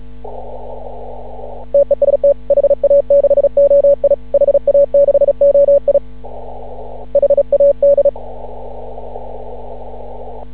Morse code, heard as a keyed tone, about 36 wpm, 560 Hz. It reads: TEST HA6OI HA6OI HAD
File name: CW / ha6oi 2